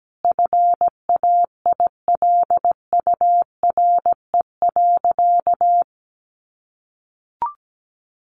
Morse code, heard as a keyed tone, about 17 wpm, 700 Hz. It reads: FAILURE.
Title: Sentences from Top 1300 Words 17wpm